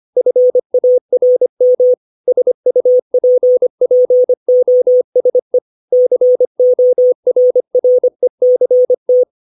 Morse code, heard as a keyed tone, about 25 wpm, 500 Hz. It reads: FARM SUPPOSE CORRECT
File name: 632 words / 632 Words 25WPM